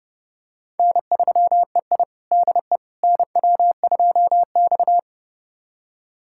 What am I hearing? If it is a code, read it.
Morse code, heard as a keyed tone, about 30 wpm, 700 Hz. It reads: N3EI DE NW2X
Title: USA Callsign DE Callsign - 4 Characters - 30wpm